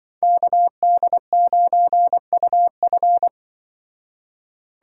Morse code, heard as a keyed tone, about 24 wpm, 700 Hz. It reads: KD9UF
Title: Speed Racing - USA 2x2 Call Signs 20wpm